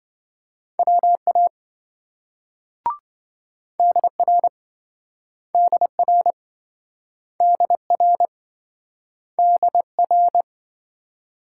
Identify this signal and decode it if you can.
Morse code, heard as a keyed tone, about 26 wpm, 700 Hz. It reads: WA DR DR DR DR